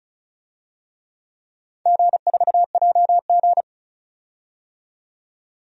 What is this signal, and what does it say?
Morse code, heard as a keyed tone, about 36 wpm, 700 Hz. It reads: G4JG
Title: DXCC-100 Call Signs - 4 Characters - 35wpm